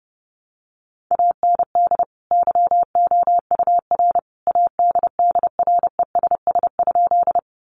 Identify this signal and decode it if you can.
Morse code, heard as a keyed tone, about 30 wpm, 700 Hz. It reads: AND YOUR ADDRESS?
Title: Sentences from Top 1000 Words 30wpm